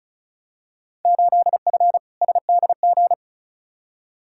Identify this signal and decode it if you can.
Morse code, heard as a keyed tone, about 36 wpm, 700 Hz. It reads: 8F SDG
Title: Field Day Exchanges Encore - 35wpm